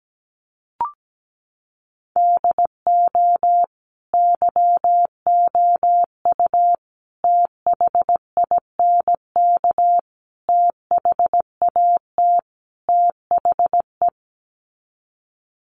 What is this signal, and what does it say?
Morse code, heard as a keyed tone, about 17 wpm, 700 Hz. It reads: DO YOU THINK THAT THE
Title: Sets of 5 Words Encore 17wpm